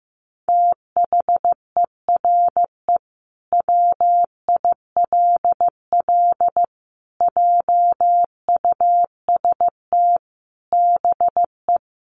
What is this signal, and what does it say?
Morse code, heard as a keyed tone, about 15 wpm, 700 Hz. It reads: THERE WILL JUST BE
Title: Sentences from Top 200 Words 15wpm